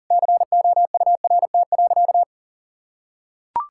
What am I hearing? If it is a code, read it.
Morse code, heard as a keyed tone, about 40 wpm, 700 Hz. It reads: COURT.
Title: Sentences from Top 1600 Words 40wpm